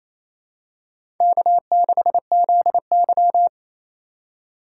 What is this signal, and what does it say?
Morse code, heard as a keyed tone, about 28 wpm, 700 Hz. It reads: K6ZY
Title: USA Call Signs - 4 Characters - 4x Character Spacing - 28wpm